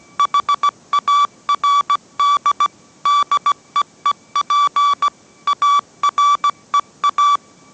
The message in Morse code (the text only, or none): HARD DEEP AREA